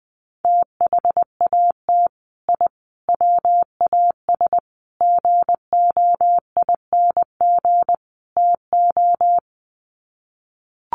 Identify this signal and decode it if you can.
Morse code, heard as a keyed tone, about 20 wpm, 700 Hz. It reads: THAT I WAS GOING TO
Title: Sets of 5 Words Encore 20wpm